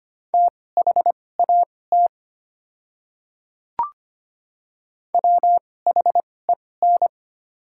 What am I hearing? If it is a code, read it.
Morse code, heard as a keyed tone, about 25 wpm, spaced as for 18 wpm, 700 Hz. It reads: THAT WHEN
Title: 2x Character Spacing - Top 100 Words - 25wpm